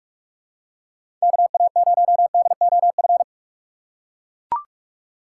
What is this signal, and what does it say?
Morse code, heard as a keyed tone, about 41 wpm, 690 Hz. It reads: KA0DOF
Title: USA Call Signs - 6 Characters - 3x Character Spacing - 45wpm